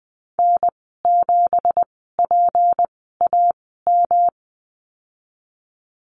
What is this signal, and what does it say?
Morse code, heard as a keyed tone, about 20 wpm, spaced as for 15 wpm, 700 Hz. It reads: N7PAM
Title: 167 Lesson 27g - Number 7 - Call Signs 20wpm